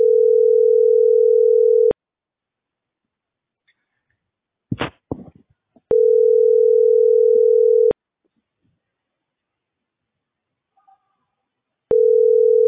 Ringback tone (North America), 440 + 480 Hz.